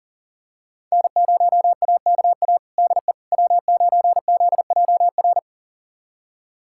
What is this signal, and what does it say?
Morse code, heard as a keyed tone, about 40 wpm, 700 Hz. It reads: N0AKA DE W9ZJR
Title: USA Callsign DE Callsign - 5 Characters - 40wpm